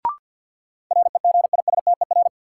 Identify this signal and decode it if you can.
Morse code, heard as a keyed tone, about 47 wpm, 700 Hz. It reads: REGISTER
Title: Sentences from Top 1200 Words - New Word Drill Alt - 50wpm